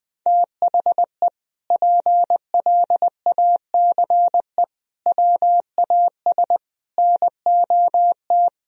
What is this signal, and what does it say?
Morse code, heard as a keyed tone, about 19 wpm, 700 Hz. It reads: THE PLACE WAS NOT
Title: Sentences from Top 1000 Words - Review Part 1 - 20wpm